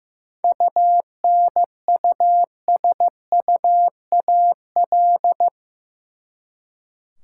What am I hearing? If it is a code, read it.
Morse code, heard as a keyed tone, about 15 wpm, 700 Hz. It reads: UNUSUAL
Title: Seven Letter Words - 15wpm